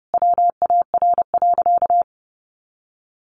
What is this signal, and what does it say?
Morse code, heard as a keyed tone, about 30 wpm, 700 Hz. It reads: WAR.
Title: Sentences from Top 400 Words 30wpm